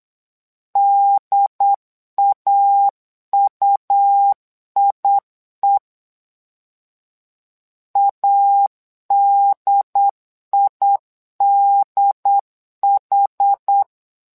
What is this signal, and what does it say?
Morse code, heard as a keyed tone, about 8 wpm, 800 Hz. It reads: DAUIE ADIDH
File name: LEZIONI / 2 UHDN / 2 UHD 8wpm 2258